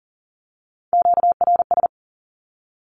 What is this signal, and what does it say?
Morse code, heard as a keyed tone, about 41 wpm, 700 Hz. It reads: QRS
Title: Q-Codes with Spoken Meaning - 40wpm